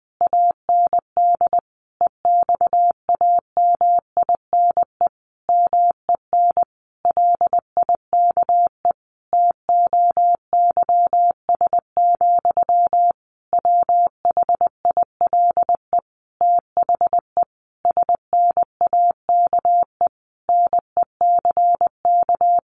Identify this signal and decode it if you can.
Morse code, heard as a keyed tone, about 20 wpm, 700 Hz. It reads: AND EXAMINE MEN LIKE TOYS, WHILE THE SNAKE NECK
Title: A Sound Of Thunder 20wpm